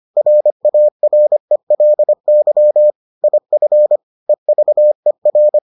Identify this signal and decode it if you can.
Morse code, heard as a keyed tone, about 25 wpm, 600 Hz. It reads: RARELY IF EVER